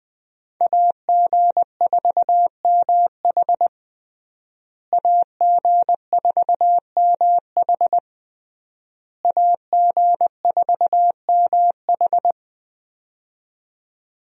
Morse code, heard as a keyed tone, about 20 wpm, 700 Hz. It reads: AG4MH AG4MH AG4MH
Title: USA Call Signs - 5 Characters - Repeated 3x - 20wpm